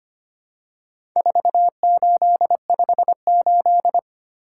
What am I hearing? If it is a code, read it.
Morse code, heard as a keyed tone, about 25 wpm, 700 Hz. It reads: 4858
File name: practice / 25wpm / Four Letter-Number Combinations - 25wpm